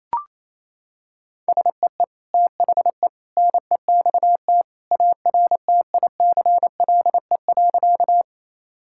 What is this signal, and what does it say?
Morse code, heard as a keyed tone, about 28 wpm, 700 Hz. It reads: SEE THE NEXT ARTICLE.
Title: Sentences from Top 1000 Words - Review Part 5 - 28wpm